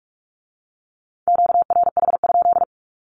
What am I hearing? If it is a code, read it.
Morse code, heard as a keyed tone, about 46 wpm, 700 Hz. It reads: QRH?
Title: Q-Codes with Spoken Meaning - 45wpm